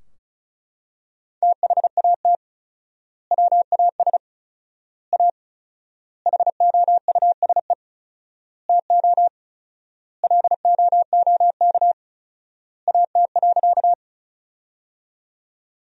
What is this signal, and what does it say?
Morse code, heard as a keyed tone, about 36 wpm, 700 Hz. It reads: THAT WAS A HOUSE TO LOOK AT.
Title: 4x Word-Spacing - Sentences from Top 200 Words - 35wpm